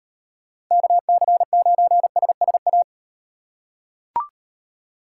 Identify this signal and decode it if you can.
Morse code, heard as a keyed tone, about 38 wpm, 700 Hz. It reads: KC9SSA